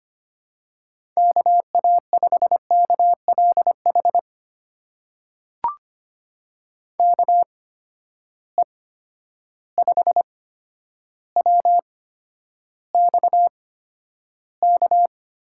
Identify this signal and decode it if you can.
Morse code, heard as a keyed tone, about 25 wpm, spaced as for 11 wpm, 700 Hz. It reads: KA5KLH KE5WXK
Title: USA Call Signs - 8x Character Spacing - 25wpm